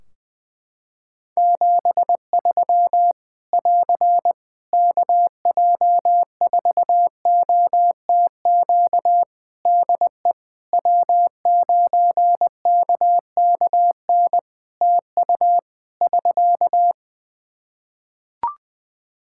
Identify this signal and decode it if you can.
Morse code, heard as a keyed tone, about 20 wpm, 700 Hz. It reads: 73 <AR> KJ4OTQ DE W9KKN TU <SK>
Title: Practice QSO - Exchange 4 - Ending - 20wpm